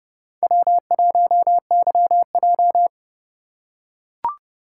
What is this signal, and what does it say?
Morse code, heard as a keyed tone, about 29 wpm, 700 Hz. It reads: W1YJ